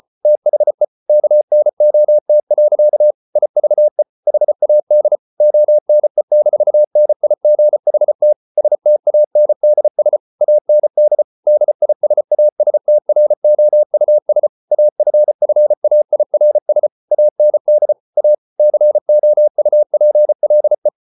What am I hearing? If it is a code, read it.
Morse code, heard as a keyed tone, about 34 wpm, 600 Hz. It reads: THE KNOT. IVE HAD ONE-NIGHT STANDS AND DISASTROUS AFFAIRS AND A COUPLE